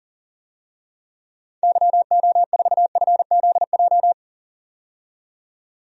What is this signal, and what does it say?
Morse code, heard as a keyed tone, about 37 wpm, 700 Hz. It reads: YO4FZJ
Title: DXCC-100 Call Signs - 6 Characters - 40wpm